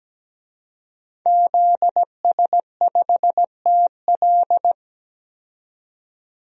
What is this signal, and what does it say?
Morse code, heard as a keyed tone, about 17 wpm, 700 Hz. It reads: ZS5TL
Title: DXCC-100 Call Signs - 5 Characters - 17wpm